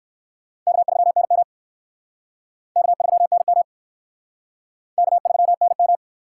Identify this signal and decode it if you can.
Morse code, heard as a keyed tone, about 69 wpm, 700 Hz. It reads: K3NM K3NM K3NM